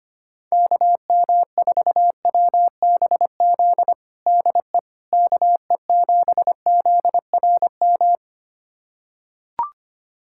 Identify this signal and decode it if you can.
Morse code, heard as a keyed tone, about 25 wpm, 700 Hz. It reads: KM4WBZ DE KE7ZRM
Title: USA Callsign DE Callsign 25wpm